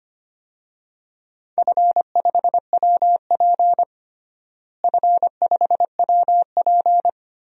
Morse code, heard as a keyed tone, about 25 wpm, 700 Hz. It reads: F5WP F5WP
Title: DXCC-100 Call Signs - 4 Characters - Repeated 3x - 25wpm